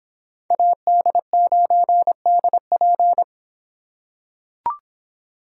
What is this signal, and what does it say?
Morse code, heard as a keyed tone, about 26 wpm, 700 Hz. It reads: AD9DP